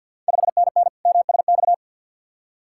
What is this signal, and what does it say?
Morse code, heard as a keyed tone, about 50 wpm, 690 Hz. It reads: 5NN MSX
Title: Ontario QSO Party Exchange 50wpm